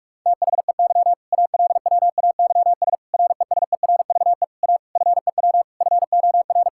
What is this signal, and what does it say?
Morse code, heard as a keyed tone, about 46 wpm, 700 Hz. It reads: THEY ALWAYS RESERVE A FEW FOR
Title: Sentences from Top 1000 Words - Review Part 6 - 45wpm